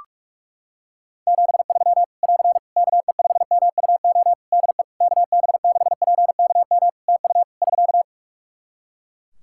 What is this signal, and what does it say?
Morse code, heard as a keyed tone, about 42 wpm, 700 Hz. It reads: 73 <AR> KE5MUO DE KB6PKM TU <SK>
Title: Practice QSO - Exchange 4 - Ending - 45wpm